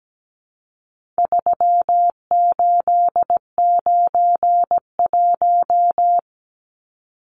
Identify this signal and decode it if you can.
Morse code, heard as a keyed tone, about 17 wpm, 700 Hz. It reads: 3891
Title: Numbers - 4-Digit 17wpm